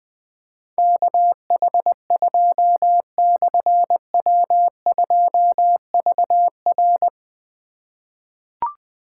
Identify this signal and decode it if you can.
Morse code, heard as a keyed tone, about 20 wpm, 700 Hz. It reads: KH2/W2VR
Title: Difficult Call Signs 20wpm